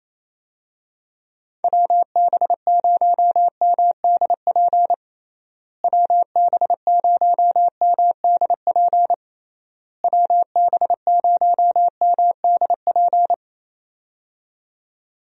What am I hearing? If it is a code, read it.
Morse code, heard as a keyed tone, about 28 wpm, 700 Hz. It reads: WB0MDP WB0MDP WB0MDP
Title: USA Call Signs - 6 Characters - Repeated 3x - 28wpm